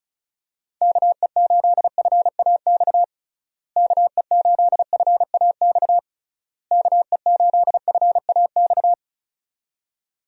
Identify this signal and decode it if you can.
Morse code, heard as a keyed tone, about 34 wpm, 700 Hz. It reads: KE8FAX KE8FAX KE8FAX
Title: USA Call Signs - 6 Characters - Repeated 3x - 35wpm